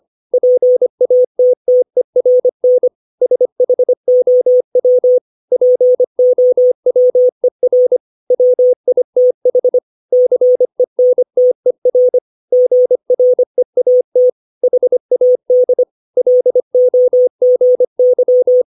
Morse code, heard as a keyed tone, about 25 wpm, 500 Hz. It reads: PATTERN SHOW POWER WITH CENTER GREAT HAD LOGY
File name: 632 words / 632 Words 25WPM